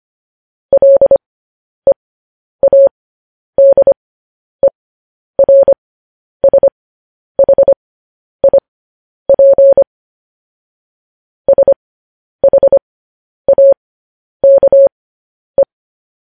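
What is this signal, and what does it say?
Morse code, heard as a keyed tone, about 25 wpm, spaced as for 10 wpm, 550 Hz. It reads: LEADERSHIP SHAKE
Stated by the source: Morse code news headlines broadcast at 10 wpm.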